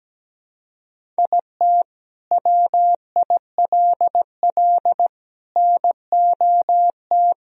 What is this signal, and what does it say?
Morse code, heard as a keyed tone, about 17 wpm, 700 Hz. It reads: IT WILL NOT
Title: Sentences from Top 1000 Words - Review Part 1 - 17wpm